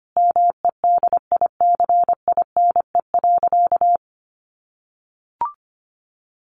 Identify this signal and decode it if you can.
Morse code, heard as a keyed tone, about 25 wpm, 700 Hz. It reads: MEDICINE.
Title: Sentences from Top 900 Words 25wpm